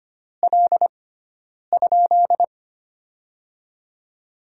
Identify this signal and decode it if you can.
Morse code, heard as a keyed tone, about 25 wpm, 700 Hz. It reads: L ?
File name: Top 90 QSO Elements - 6x Character Spacing - 25wpm